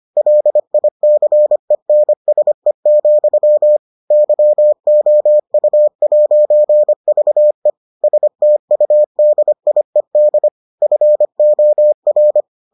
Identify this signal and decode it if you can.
Morse code, heard as a keyed tone, about 25 wpm, 600 Hz. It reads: LICENSE, YOU'VE STUDIED FOR